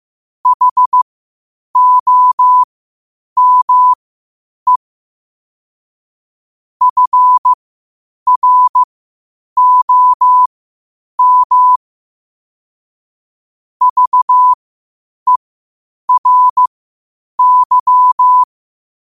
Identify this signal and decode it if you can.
Morse code, heard as a keyed tone, about 15 wpm, spaced as for 8 wpm, 1000 Hz. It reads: HOME FROM VERY
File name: qotdfiles / 8wpm